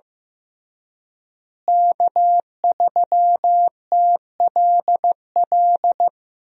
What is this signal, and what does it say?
Morse code, heard as a keyed tone, about 15 wpm, 700 Hz. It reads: K3TLL
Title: USA Call Signs - 5 Characters - Repeated 3x - 15wpm